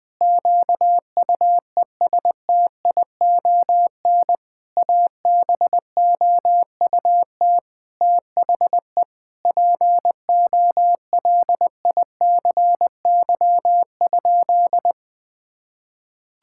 Morse code, heard as a keyed tone, about 20 wpm, 700 Hz. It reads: QUESTION ABOUT THE POLICY?
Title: Sentences with 6-Word Phrases - 20wpm